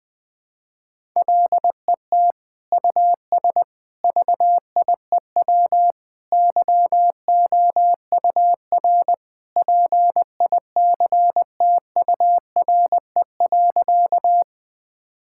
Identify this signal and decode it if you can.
Morse code, heard as a keyed tone, about 20 wpm, 700 Hz. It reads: LET US VIEW YOUR PICTURE.
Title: Sentences from Top 1000 Words - Review Part 6 - 20wpm